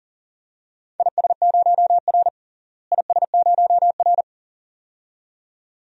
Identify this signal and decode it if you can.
Morse code, heard as a keyed tone, about 40 wpm, 700 Hz. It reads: IS0R IS0R
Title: DXCC-100 Call Signs - 4 Characters - Repeated 2x - 40wpm